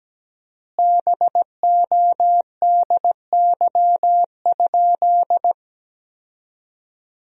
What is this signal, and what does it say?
Morse code, heard as a keyed tone, about 17 wpm, 700 Hz. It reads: BODY?
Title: 4x Word-Spacing - Sentences from Top 400 Words - 17wpm